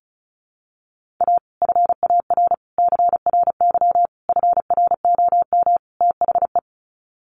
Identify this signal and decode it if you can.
Morse code, heard as a keyed tone, about 34 wpm, 700 Hz. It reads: A FAR CRY FROM THE
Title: Sets of 5 Words Encore 35wpm